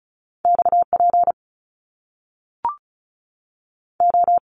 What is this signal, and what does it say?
Morse code, heard as a keyed tone, about 36 wpm, 700 Hz. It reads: X P O
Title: Top Two Letter Combinations in Words 35wpm